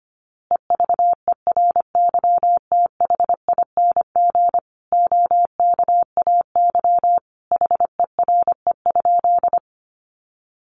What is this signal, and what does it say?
Morse code, heard as a keyed tone, about 25 wpm, 700 Hz. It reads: EVERYTHING OKAY HERE?
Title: A Sound Of Thunder 25wpm